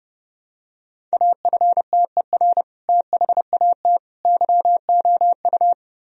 Morse code, heard as a keyed tone, about 29 wpm, 700 Hz. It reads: AFTER THAT YOU
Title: Sentences from Top 1000 Words - Review Part 2 - 30wpm